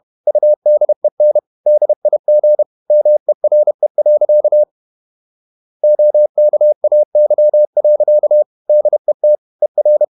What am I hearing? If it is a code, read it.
Morse code, heard as a keyed tone, about 31 wpm, 600 Hz. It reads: UDEN DIG MERE. OKAY. DET ER
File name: Ferie QRQ 31wpm